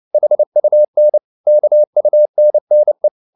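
Morse code, heard as a keyed tone, about 29 wpm, 600 Hz. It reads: HUN KUNNE
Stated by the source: Fast (QRQ) Morse at 29 wpm, from CW med Gnister.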